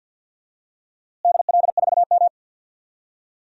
Morse code, heard as a keyed tone, about 52 wpm, 700 Hz. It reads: DL4M
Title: DXCC-100 Call Signs - 4 Characters - Repeated 2x - 50wpm